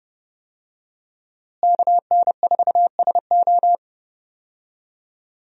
Morse code, heard as a keyed tone, about 29 wpm, 700 Hz. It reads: KN4SO
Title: USA Call Signs - 5 Characters - Repeated 3x - 30wpm